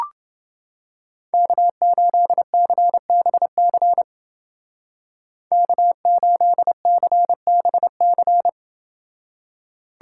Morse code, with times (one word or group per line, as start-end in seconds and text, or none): 1.3-4.0 K8CBC
5.5-8.5 K8CBC